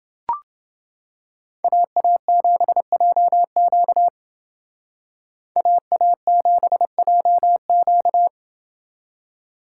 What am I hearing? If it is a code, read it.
Morse code, heard as a keyed tone, about 28 wpm, 700 Hz. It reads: AA7JQ AA7JQ